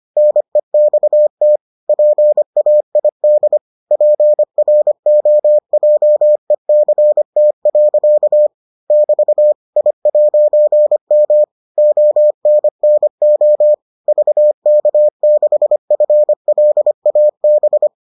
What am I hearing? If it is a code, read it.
Morse code, heard as a keyed tone, about 25 wpm, 600 Hz. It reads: NEXT PAID PROJECT. <BT> I'M ONNO VK6FLAB